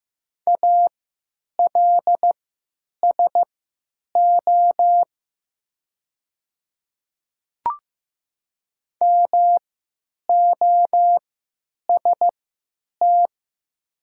Morse code, heard as a keyed tone, about 15 wpm, spaced as for 8 wpm, 700 Hz. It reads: ALSO MOST
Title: 3x Character Spacing - Top 200 Words - 15wpm